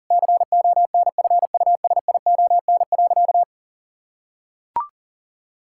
Morse code, heard as a keyed tone, about 40 wpm, 700 Hz. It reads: CONFUSION.
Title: Sentences from Top 1600 Words 40wpm